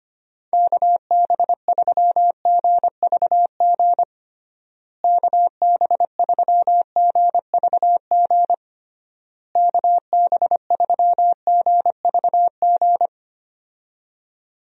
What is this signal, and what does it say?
Morse code, heard as a keyed tone, about 25 wpm, 700 Hz. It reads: KB3GVG KB3GVG KB3GVG
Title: USA Call Signs - 6 Characters - Repeated 3x - 25wpm